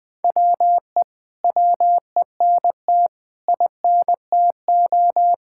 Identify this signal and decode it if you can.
Morse code, heard as a keyed tone, about 20 wpm, 700 Hz. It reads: WE WENT INTO
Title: Sentences from Top 1600 Words 20wpm